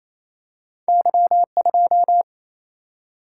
Morse code, heard as a keyed tone, about 28 wpm, 700 Hz. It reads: Y2
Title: Single-Letter Single-Number 28wpm